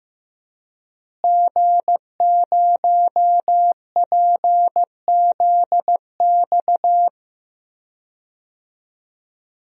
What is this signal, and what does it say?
Morse code, heard as a keyed tone, about 16 wpm, 700 Hz. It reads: G0PZX
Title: DXCC-100 Call Signs - 5 Characters - Repeated 3x - 15wpm